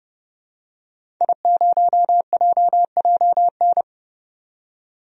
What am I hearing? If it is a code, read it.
Morse code, heard as a keyed tone, about 30 wpm, 700 Hz. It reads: I0JJN
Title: DXCC-100 Call Signs - 5 Characters - 30wpm